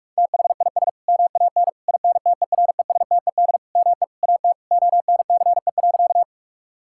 Morse code, heard as a keyed tone, about 46 wpm, 700 Hz. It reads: THIS MAN INTERESTED ME AT ONCE.
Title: Sentences from Top 1000 Words - Review Part 6 - 45wpm